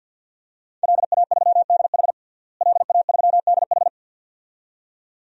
Morse code, heard as a keyed tone, about 48 wpm, 700 Hz. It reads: LA3DH LA3DH
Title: DXCC-100 Call Signs - 5 Characters - Repeated 2x - 50wpm